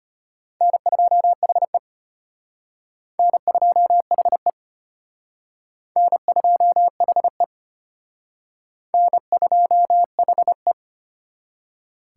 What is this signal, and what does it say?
Morse code, heard as a keyed tone, about 31 wpm, 700 Hz. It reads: N2HE N2HE N2HE N2HE
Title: Speed Racing - USA 1x2 Call Signs 25wpm